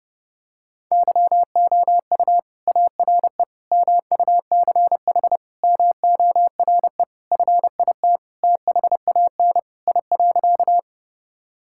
Morse code, heard as a keyed tone, about 30 wpm, 700 Hz. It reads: YOU ARE MUCH MORE FIT THAN I.
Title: Sentences from Top 1000 Words - Review Part 5 - 30wpm